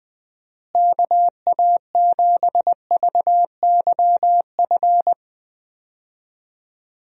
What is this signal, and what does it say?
Morse code, heard as a keyed tone, about 20 wpm, 700 Hz. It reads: KA7VYF
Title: USA Call Signs - 6 Characters - Repeated 3x - 20wpm